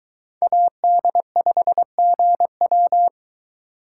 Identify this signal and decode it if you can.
Morse code, heard as a keyed tone, about 23 wpm, 700 Hz. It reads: AD5GW